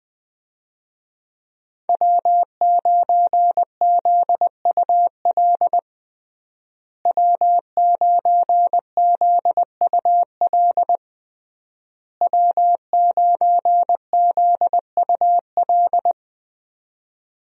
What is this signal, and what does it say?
Morse code, heard as a keyed tone, about 20 wpm, 700 Hz. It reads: W9ZUL W9ZUL W9ZUL
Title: USA Call Signs - 5 Characters - Repeated 3x - 20wpm